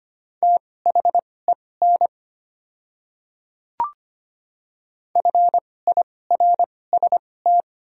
Morse code, heard as a keyed tone, about 25 wpm, spaced as for 18 wpm, 700 Hz. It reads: THEN FIRST
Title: 2x Character Spacing - Top 100 Words - 25wpm